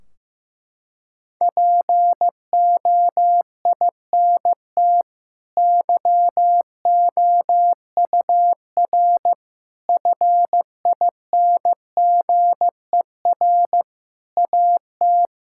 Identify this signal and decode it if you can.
Morse code, heard as a keyed tone, about 15 wpm, 700 Hz. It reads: POINT YOUR FINGER AT
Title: Sentences from Top 1600 Words 15wpm